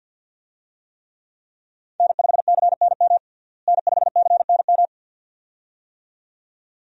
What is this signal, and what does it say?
Morse code, heard as a keyed tone, about 46 wpm, 700 Hz. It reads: N5CNM N5CNM
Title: USA Call Signs - 5 Characters - Repeated 2x - 50wpm